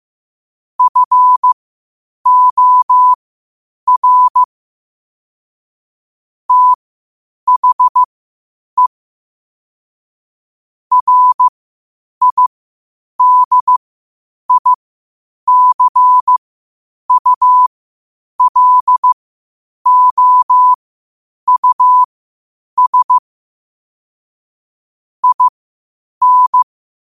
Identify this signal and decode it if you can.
Morse code, heard as a keyed tone, about 15 wpm, spaced as for 8 wpm, 1000 Hz. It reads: FOR THE RIDICULOUS IN
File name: qotdfiles / 8wpm